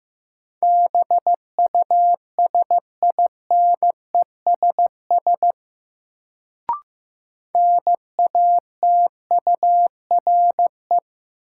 Morse code, heard as a keyed tone, about 15 wpm, 700 Hz. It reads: BUSINESS NATURE
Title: Sentences from Top 600 Words - New Word Drill - 15wpm